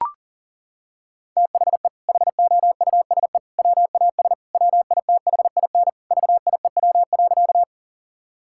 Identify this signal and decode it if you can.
Morse code, heard as a keyed tone, about 40 wpm, 700 Hz. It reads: THE HOUSE WAS WITHIN VIEW.
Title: Sentences from Top 1000 Words - Review Part 6 - 40wpm